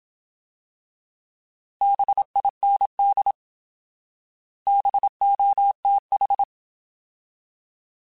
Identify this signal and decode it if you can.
Morse code, heard as a keyed tone, about 27 wpm, 800 Hz. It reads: BIND BOTH
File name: Alphabetical / Alphabetical mp3s / Bs @25